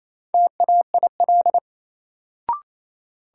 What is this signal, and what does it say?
Morse code, heard as a keyed tone, about 28 wpm, 700 Hz. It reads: TAIL
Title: Sentences from Top 1000 Words - Review Part 3 - New Word Drill - 28wpm